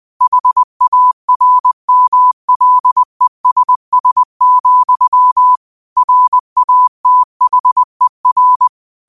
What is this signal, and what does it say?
Morse code, heard as a keyed tone, about 20 wpm, 1000 Hz. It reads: HARMLESS, RATHER
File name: qotdfiles / 20wpm